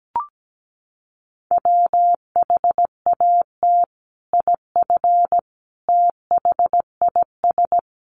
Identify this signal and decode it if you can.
Morse code, heard as a keyed tone, about 17 wpm, 700 Hz. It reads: WHAT IF THIS
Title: Sentences from Top 1500 Words 17wpm